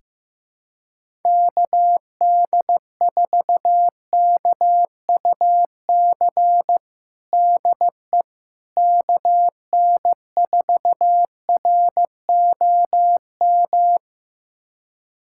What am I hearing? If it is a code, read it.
Morse code, heard as a keyed tone, about 15 wpm, 700 Hz. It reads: KD4KUC DE KN4ROM
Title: USA Callsign DE Callsign 15wpm